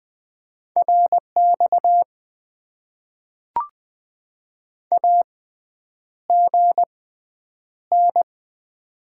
Morse code, heard as a keyed tone, about 20 wpm, spaced as for 9 wpm, 700 Hz. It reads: RX AGN
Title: Top 90 QSO Elements - 6x Character Spacing - 20wpm